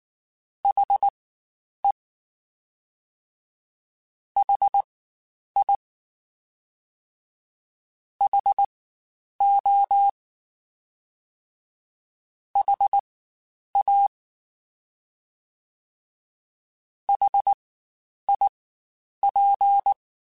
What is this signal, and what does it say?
Morse code, heard as a keyed tone, about 19 wpm, spaced as for 9 wpm, 800 Hz. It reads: HE HI HO HA HIP
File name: Spaced Alphas / Spaced Alphas mp3s / Hs sp @18